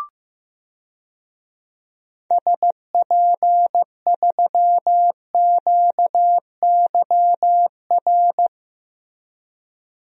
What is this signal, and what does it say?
Morse code, heard as a keyed tone, about 15 wpm, 700 Hz. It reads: SP3QYR
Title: DXCC-100 Call Signs - 6 Characters - Repeated 3x - 15wpm